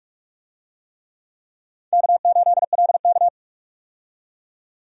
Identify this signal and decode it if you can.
Morse code, heard as a keyed tone, about 46 wpm, 700 Hz. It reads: K8LK
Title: USA Call Signs - 4 Characters - 45wpm